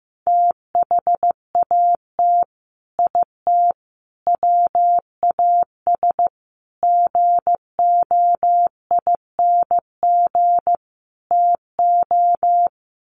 Morse code, heard as a keyed tone, about 15 wpm, 700 Hz. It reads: THAT IT WAS GOING TO
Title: Sets of 5 Words Encore2 15wpm